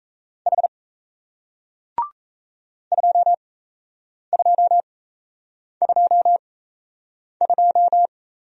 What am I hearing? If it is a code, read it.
Morse code, heard as a keyed tone, about 36 wpm, spaced as for 8 wpm, 700 Hz. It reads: H 2222